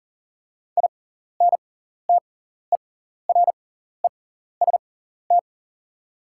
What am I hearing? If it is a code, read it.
Morse code, heard as a keyed tone, about 37 wpm, spaced as for 14 wpm, 690 Hz. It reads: INTEREST
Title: 6x Character Spacing - Top 500 Words - 40wpm